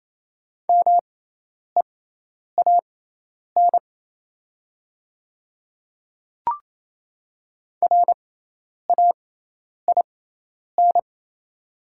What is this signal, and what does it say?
Morse code, heard as a keyed tone, about 28 wpm, spaced as for 10 wpm, 700 Hz. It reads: MEAN RAIN
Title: 6x Character Spacing - Top 300 Words - 28wpm